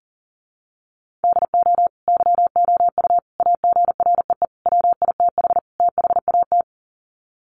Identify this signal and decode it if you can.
Morse code, heard as a keyed tone, about 40 wpm, 700 Hz. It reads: DO YOU AGREE WITH THAT
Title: Sets of 5 Words Encore 40wpm